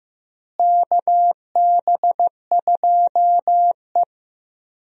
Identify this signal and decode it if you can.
Morse code, heard as a keyed tone, about 15 wpm, 700 Hz. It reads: KB2E
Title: USA Call Signs - 4 Characters - Repeated 3x - 15wpm